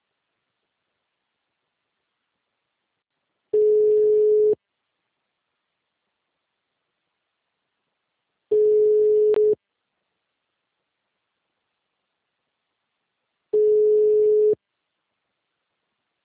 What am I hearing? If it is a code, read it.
Ringback tone (Europe, 425 Hz).
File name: out- -2003- - - .132